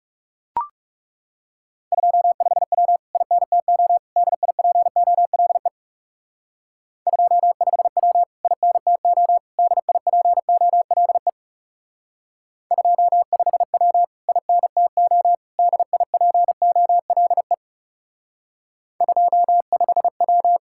Morse code, heard as a keyed tone, about 38 wpm, 700 Hz. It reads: 25W INTO DIPOLE 25W INTO DIPOLE 25W INTO DIPOLE 25W